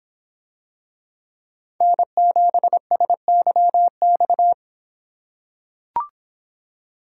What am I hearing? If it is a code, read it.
Morse code, heard as a keyed tone, about 25 wpm, 700 Hz. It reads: N7SYX